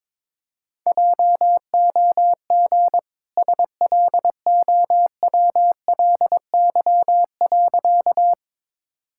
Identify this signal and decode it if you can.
Morse code, heard as a keyed tone, about 22 wpm, 700 Hz. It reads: JOG SLOWLY.
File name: practice / 22wpm / Easy Sentences with Hints - 22wpm